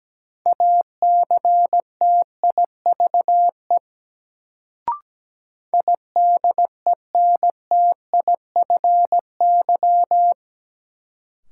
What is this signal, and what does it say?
Morse code, heard as a keyed tone, about 17 wpm, 700 Hz. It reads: ACTIVE IDENTIFY
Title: Sentences from Top 900 Words - New Word Drill - 17wpm